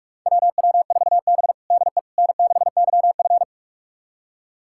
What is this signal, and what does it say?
Morse code, heard as a keyed tone, about 46 wpm, 700 Hz. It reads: WW4B DE N6YF
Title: USA Callsign DE Callsign - 4 Characters - 45wpm